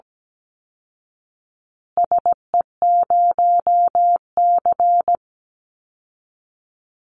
Morse code, heard as a keyed tone, about 17 wpm, 700 Hz. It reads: SE0C
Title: Call Signs Repeated 2x - 17wpm